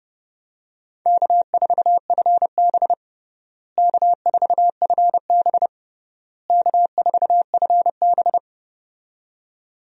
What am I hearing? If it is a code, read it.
Morse code, heard as a keyed tone, about 30 wpm, 700 Hz. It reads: K4FB K4FB K4FB
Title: USA Call Signs - 4 Characters - Repeated 3x - 30wpm